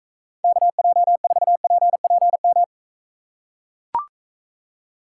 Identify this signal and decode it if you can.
Morse code, heard as a keyed tone, about 44 wpm, 700 Hz. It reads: KJ4PPM